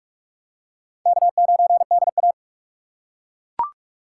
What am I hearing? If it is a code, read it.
Morse code, heard as a keyed tone, about 41 wpm, 700 Hz. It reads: K9DA